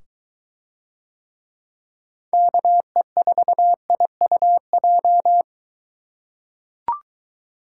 Morse code, heard as a keyed tone, about 22 wpm, 700 Hz. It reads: KE4IUJ